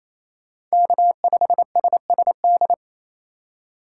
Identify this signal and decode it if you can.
Morse code, heard as a keyed tone, about 36 wpm, 700 Hz. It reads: K5SSD